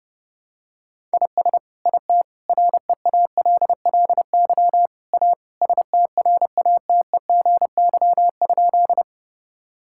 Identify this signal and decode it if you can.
Morse code, heard as a keyed tone, about 31 wpm, 700 Hz. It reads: IS IT REALLY A STRATEGY?
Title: Sentences from Top 1000 Words - Review Part 5 - 30wpm